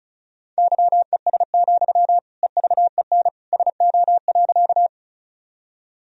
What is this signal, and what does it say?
Morse code, heard as a keyed tone, about 34 wpm, 700 Hz. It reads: YES, EVEN SO.
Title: Sentences from Top 1000 Words - Review Part 1 - 35wpm